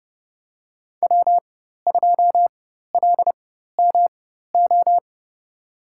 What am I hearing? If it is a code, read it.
Morse code, heard as a keyed tone, about 30 wpm, spaced as for 14 wpm, 700 Hz. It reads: W2LMO
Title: USA Call Signs - 5 Characters - 4x Character Spacing - 30wpm